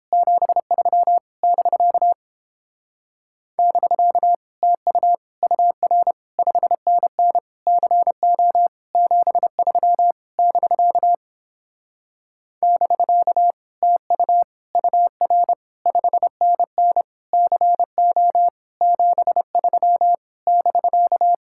Morse code, heard as a keyed tone, about 29 wpm, 700 Hz. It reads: 73 <BK> <BK> TU UR 5NN CO 73 <BK> <BK> TU UR 5NN CO 73 <BK>